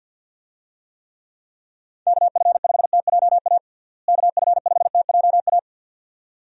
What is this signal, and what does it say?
Morse code, heard as a keyed tone, about 51 wpm, 700 Hz. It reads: KF5TJA KF5TJA
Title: USA Call Signs - 6 Characters - Repeated 2x - 50wpm